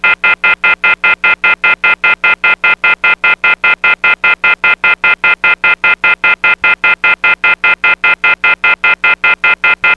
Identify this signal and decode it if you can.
Off-hook warning tone, 1400 + 2060 + 2450 + 2600 Hz, with noise.